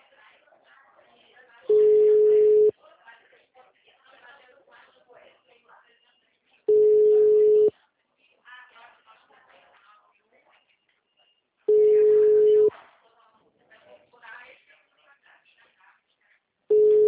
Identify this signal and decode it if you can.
Ringback tone (Europe, 425 Hz), with speech.
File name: out- -2012- - - .9592